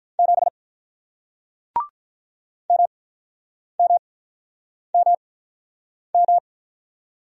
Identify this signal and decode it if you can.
Morse code, heard as a keyed tone, about 47 wpm, spaced as for 9 wpm, 700 Hz. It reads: 7 MMMM